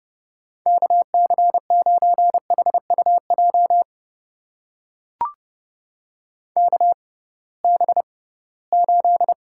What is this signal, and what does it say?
Morse code, heard as a keyed tone, about 32 wpm, spaced as for 20 wpm, 700 Hz. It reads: KC9HUJ KB8
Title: USA Call Signs - 6 Characters - 6x Character Spacing - 30wpm